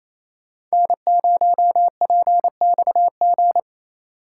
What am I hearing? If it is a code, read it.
Morse code, heard as a keyed tone, about 35 wpm, 700 Hz. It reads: N0PXG